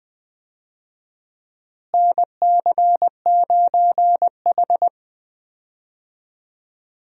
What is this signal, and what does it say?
Morse code, heard as a keyed tone, about 20 wpm, 700 Hz. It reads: NC9H
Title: USA Call Signs - 4 Characters - Repeated 3x - 20wpm